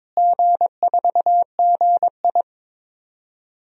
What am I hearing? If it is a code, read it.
Morse code, heard as a keyed tone, about 22 wpm, 700 Hz. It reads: G4GI
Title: DXCC-100 Call Signs - 4 Characters - Repeated 3x - 22wpm